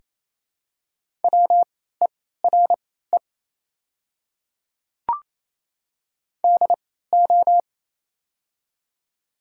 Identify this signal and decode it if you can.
Morse code, heard as a keyed tone, about 28 wpm, spaced as for 16 wpm, 700 Hz. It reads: WERE DO
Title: 3x Character Spacing - Top 200 Words - 28wpm